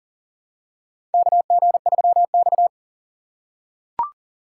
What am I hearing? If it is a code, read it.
Morse code, heard as a keyed tone, about 40 wpm, 700 Hz. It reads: KG3X
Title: USA Call Signs - 4 Characters - 6x Character Spacing - 40wpm